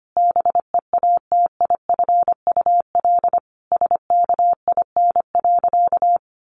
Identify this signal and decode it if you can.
Morse code, heard as a keyed tone, about 25 wpm, 700 Hz. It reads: BEATIFUL SKIN.
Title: Sentences from Top 1300 Words 25wpm